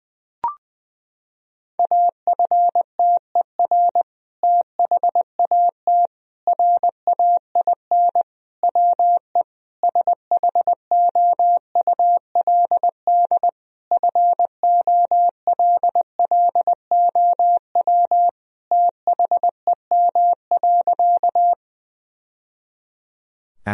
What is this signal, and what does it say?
Morse code, heard as a keyed tone, about 21 wpm, 700 Hz. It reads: AFTER THAT RAIN WE SHOULD FOLLOW THEM.
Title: Sentences from Top 1000 Words - Review Part 1 - 20wpm